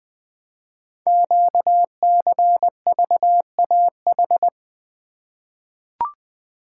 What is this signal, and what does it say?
Morse code, heard as a keyed tone, about 20 wpm, 700 Hz. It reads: QCVAH
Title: Code Groups - 5 Letters - 20wpm